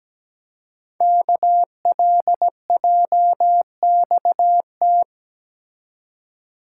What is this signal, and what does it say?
Morse code, heard as a keyed tone, about 17 wpm, 700 Hz. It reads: KLJXT
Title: Code Groups - 5 Letter:Numbers - 17wpm